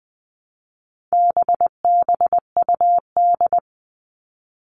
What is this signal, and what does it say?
Morse code, heard as a keyed tone, about 20 wpm, 700 Hz. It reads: BBUD
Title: All Cut Numbers - 4-Digit 20wpm